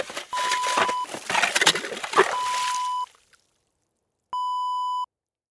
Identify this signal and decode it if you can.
There is no code.